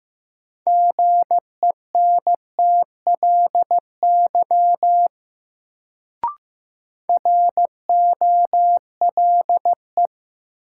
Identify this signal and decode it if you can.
Morse code, heard as a keyed tone, about 15 wpm, 700 Hz. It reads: GENTLY ROLE
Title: Sentences from Top 1500 Words - New Word Drill Alt - 15wpm